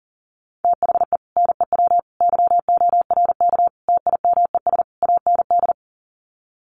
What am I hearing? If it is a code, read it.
Morse code, heard as a keyed tone, about 40 wpm, 700 Hz. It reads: THE NEW YORK TIMES AND
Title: Sets of 5 Words Encore 40wpm